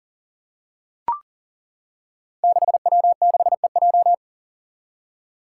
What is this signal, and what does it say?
Morse code, heard as a keyed tone, about 39 wpm, 700 Hz. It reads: 6W6EJ
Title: Five Challenging Letter-Number Combinations 40wpm